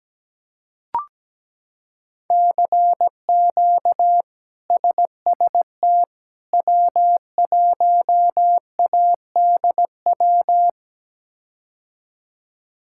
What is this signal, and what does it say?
Morse code, heard as a keyed tone, about 17 wpm, 700 Hz. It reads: CQ SST W1ADW
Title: K1USN SST QSO Practice - 17wpm